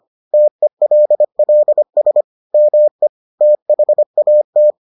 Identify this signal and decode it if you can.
Morse code, heard as a keyed tone, about 25 wpm, 600 Hz. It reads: TELLS ME THAT